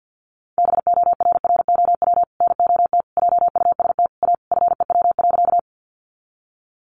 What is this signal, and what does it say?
Morse code, heard as a keyed tone, about 50 wpm, 700 Hz. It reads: BORROW NOT JUST A FEW.
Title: Sentences from Top 1500 Words 50wpm